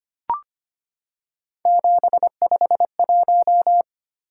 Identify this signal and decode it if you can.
Morse code, heard as a keyed tone, about 25 wpm, 700 Hz. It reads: 751
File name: Three Letter-Number Combinations - 25wpm